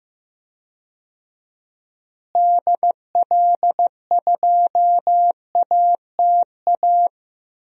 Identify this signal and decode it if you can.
Morse code, heard as a keyed tone, about 15 wpm, 700 Hz. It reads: DL2ATA
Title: DXCC-100 Call Signs - 6 Characters - Repeated 2x - 15wpm